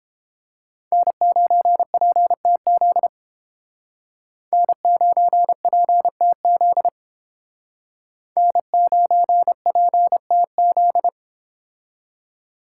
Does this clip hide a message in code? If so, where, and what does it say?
Yes: Morse code from 0.9 to 11.1 s, heard as a keyed tone, about 31 wpm, 700 Hz. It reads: N9PTZ N9PTZ N9PTZ